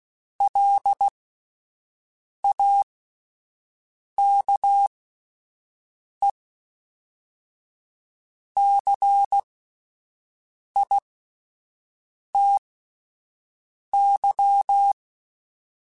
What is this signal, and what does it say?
Morse code, heard as a keyed tone, about 16 wpm, spaced as for 5 wpm, 800 Hz. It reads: LAKECITY